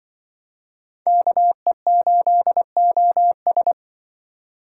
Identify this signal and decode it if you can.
Morse code, heard as a keyed tone, about 25 wpm, 700 Hz. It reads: KE8OS